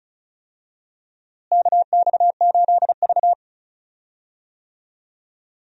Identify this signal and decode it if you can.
Morse code, heard as a keyed tone, about 36 wpm, 700 Hz. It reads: KX8V
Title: USA Call Signs - 4 Characters - Repeated 2x - 35wpm